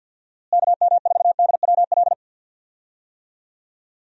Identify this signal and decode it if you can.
Morse code, heard as a keyed tone, about 46 wpm, 700 Hz. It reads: KM4DWL
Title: USA Call Signs - 6 Characters - Repeated 2x - 50wpm